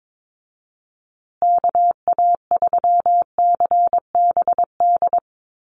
Morse code, heard as a keyed tone, about 22 wpm, 700 Hz. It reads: KA3CBD
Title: USA Call Signs Repeated 3x 22wpm